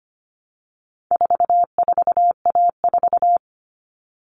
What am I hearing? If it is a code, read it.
Morse code, heard as a keyed tone, about 25 wpm, 700 Hz. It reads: 44A4
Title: All Cut Numbers - 4-Digit 25wpm